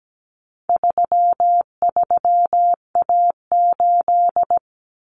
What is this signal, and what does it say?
Morse code, heard as a keyed tone, about 17 wpm, 700 Hz. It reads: 33A8
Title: Common Cut Numbers - 4-Digit 17wpm